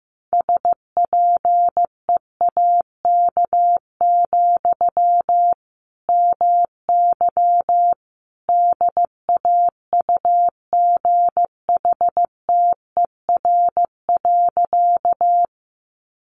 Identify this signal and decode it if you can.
Morse code, heard as a keyed tone, about 15 wpm, 700 Hz. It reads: SPEAK, MY DAUGHTER.
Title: Sentences from Top 1500 Words 15wpm